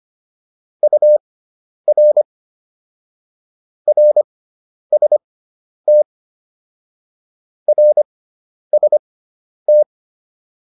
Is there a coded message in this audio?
Yes: Morse code from 0.8 to 9.8 s, heard as a keyed tone, about 25 wpm, spaced as for 10 wpm, 600 Hz. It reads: UR RST RST